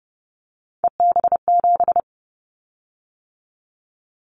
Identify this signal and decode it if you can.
Morse code, heard as a keyed tone, about 30 wpm, 700 Hz. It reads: EB7
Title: Callsign Prefixes for Top 25 DXCC - 30wpm